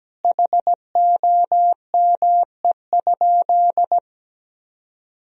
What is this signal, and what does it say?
Morse code, heard as a keyed tone, about 17 wpm, 700 Hz. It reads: HOME?
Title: Sentences from Top 1000 Words - Review Part 4 - 17wpm